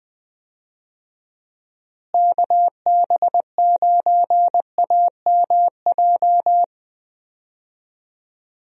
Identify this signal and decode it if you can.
Morse code, heard as a keyed tone, about 20 wpm, 700 Hz. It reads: KB9AMJ
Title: USA Call Signs - 6 Characters - Repeated 3x - 20wpm